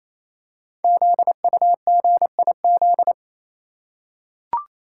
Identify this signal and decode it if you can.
Morse code, heard as a keyed tone, about 28 wpm, 700 Hz. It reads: ZUGIZ
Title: Code Groups - 5 Letter:Numbers - 28wpm